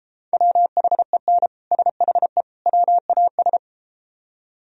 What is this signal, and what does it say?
Morse code, heard as a keyed tone, about 32 wpm, 700 Hz. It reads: WHEN SHE WAS